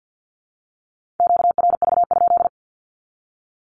Morse code, heard as a keyed tone, about 49 wpm, 700 Hz. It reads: QRV?
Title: Q-Codes 50wpm